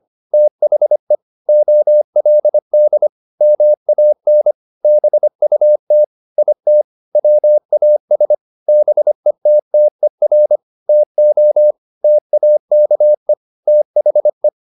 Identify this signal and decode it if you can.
Morse code, heard as a keyed tone, about 25 wpm, 600 Hz. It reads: THE OLD MAN BUT IT WAS BETTER TO TAKE THE
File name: the old man and the sea part 1 25wpm